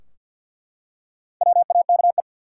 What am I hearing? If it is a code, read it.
Morse code, heard as a keyed tone, about 50 wpm, 700 Hz. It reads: WAKE
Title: Four Letter Words - 50wpm